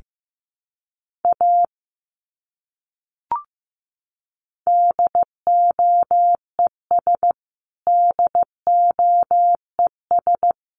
Morse code, heard as a keyed tone, about 15 wpm, 700 Hz. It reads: A DOES DOES
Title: Top 200 Words Repeated 15wpm